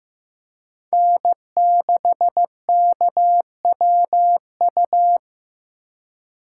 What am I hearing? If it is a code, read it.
Morse code, heard as a keyed tone, about 15 wpm, 700 Hz. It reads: N6KWU
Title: USA Call Signs - 5 Characters - 4x Character Spacing - 15wpm